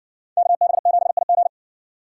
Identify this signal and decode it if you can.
Morse code, heard as a keyed tone, about 60 wpm, 700 Hz. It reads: KB7IG